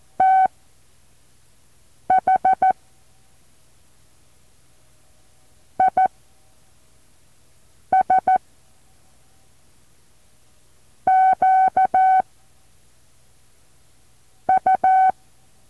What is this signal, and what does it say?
Morse code, heard as a keyed tone, about 14 wpm, spaced as for 4 wpm, 760 Hz. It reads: THIS QU
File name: intl morse code2